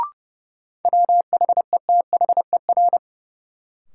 Morse code, heard as a keyed tone, about 31 wpm, 700 Hz. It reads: WHETHER
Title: Sentences from Top 1000 Words - Review Part 3 - New Word Drill - 30wpm